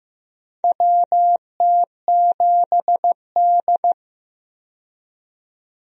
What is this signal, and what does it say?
Morse code, heard as a keyed tone, about 15 wpm, 700 Hz. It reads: WT7D
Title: USA Call Signs - 4 Characters - 15wpm